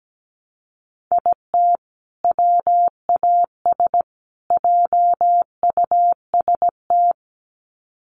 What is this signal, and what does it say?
Morse code, heard as a keyed tone, about 17 wpm, 700 Hz. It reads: IT WAS JUST
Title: Sets of 3 Words 17wpm